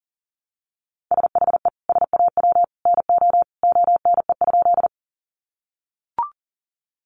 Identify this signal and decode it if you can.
Morse code, heard as a keyed tone, about 40 wpm, 700 Hz. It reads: SHE SAW NO ONE?
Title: Sentences from Top 300 Words 40wpm